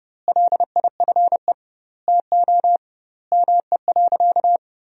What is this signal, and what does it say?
Morse code, heard as a keyed tone, about 32 wpm, 700 Hz. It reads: LIFE TO ME.
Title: 2x Word-Spacing - Sentences from Top 500 Words - 30wpm